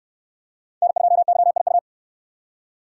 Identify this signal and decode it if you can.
Morse code, heard as a keyed tone, about 68 wpm, 700 Hz. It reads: N2YIU